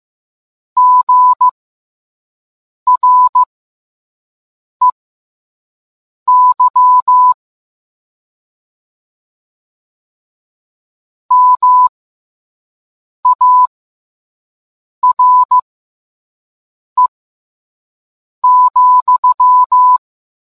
Morse code, heard as a keyed tone, about 15 wpm, spaced as for 5 wpm, 1000 Hz. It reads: GREY MARE,